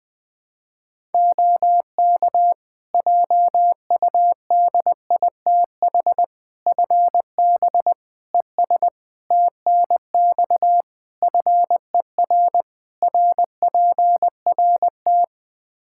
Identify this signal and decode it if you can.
Morse code, heard as a keyed tone, about 21 wpm, 700 Hz. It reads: OK JUDITH FB ES TNX FER RPRT
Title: Practice QSO - Exchange 2 - Info - 20wpm